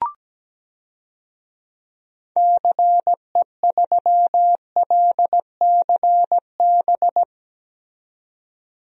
Morse code, heard as a keyed tone, about 17 wpm, 700 Hz. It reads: CE3LCB
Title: DXCC-100 Call Signs - 6 Characters - 17wpm